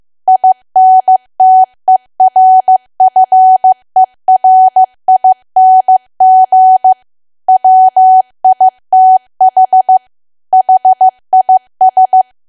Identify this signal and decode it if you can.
Morse code, heard as a keyed tone, about 15 wpm, 750 Hz. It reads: INTERFERING WITH HIS
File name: cwa / W1AW-Level2 / 15WPM